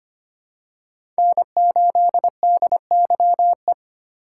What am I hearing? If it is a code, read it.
Morse code, heard as a keyed tone, about 25 wpm, 700 Hz. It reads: N8DYE
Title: USA Call Signs - 5 Characters - 8x Character Spacing - 25wpm